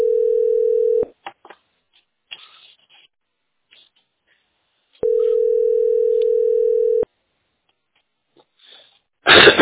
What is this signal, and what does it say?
Ringback tone (North America), 440 + 480 Hz.